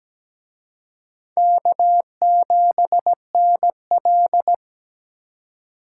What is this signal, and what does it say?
Morse code, heard as a keyed tone, about 20 wpm, 700 Hz. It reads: K7NL
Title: Speed Racing - USA 1x2 Call Signs 17wpm